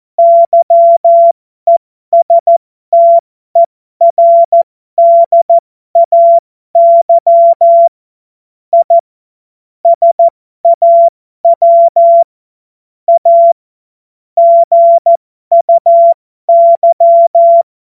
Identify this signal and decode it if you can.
Morse code, heard as a keyed tone, about 14 wpm, spaced as for 12 wpm, 680 Hz. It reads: YESTERDAY I SAW A GUY
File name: Morsekurs / Witze